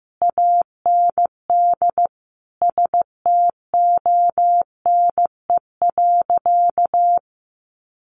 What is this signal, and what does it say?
Morse code, heard as a keyed tone, about 15 wpm, 700 Hz. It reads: AND STONE.
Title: A Sound Of Thunder 15wpm